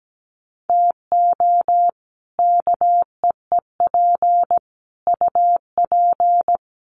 Morse code, heard as a keyed tone, about 17 wpm, 700 Hz. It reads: TO KEEP UP
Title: Sets of 5 Words Encore 17wpm